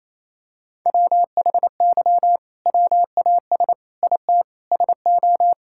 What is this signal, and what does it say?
Morse code, heard as a keyed tone, about 28 wpm, 700 Hz. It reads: WHY WAS IT SO
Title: Sentences from Top 1000 Words - Review Part 5 - 28wpm